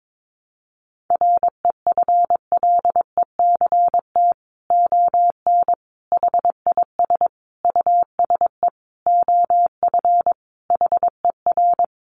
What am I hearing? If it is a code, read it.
Morse code, heard as a keyed tone, about 22 wpm, 700 Hz. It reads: REFLECT ON HIS USE OF HER
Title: Sentences from Top 1500 Words 22wpm